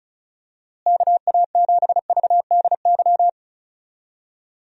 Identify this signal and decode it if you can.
Morse code, heard as a keyed tone, about 34 wpm, 700 Hz. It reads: KA7VDY
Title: USA Call Signs - 6 Characters - Repeated 2x - 35wpm